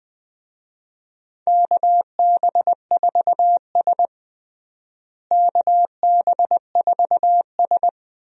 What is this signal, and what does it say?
Morse code, heard as a keyed tone, about 20 wpm, 700 Hz. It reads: KB4S KB4S
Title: USA Call Signs - 4 Characters - Repeated 2x - 20wpm